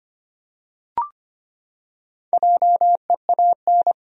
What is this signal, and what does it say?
Morse code, heard as a keyed tone, about 25 wpm, 700 Hz. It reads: JEAN
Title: USA Common Names - Max 4 Letters - 25wpm